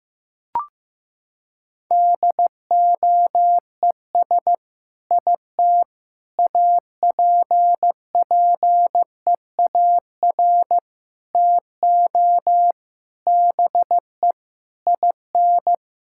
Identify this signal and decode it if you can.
Morse code, heard as a keyed tone, about 15 wpm, 700 Hz. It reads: DOES IT APPEAR TO BE IN
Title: Sentences from Top 1000 Words - Review Part 6 - 15wpm